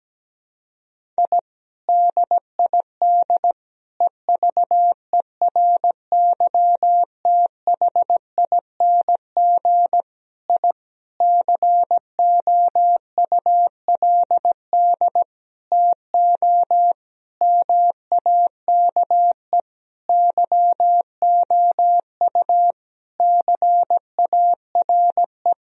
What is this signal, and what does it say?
Morse code, heard as a keyed tone, about 17 wpm, 700 Hz. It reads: I DID EVERYTHING I COULD TO MAKE YOU CARE
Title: Sentences from Top 1000 Words - Review Part 1 - 17wpm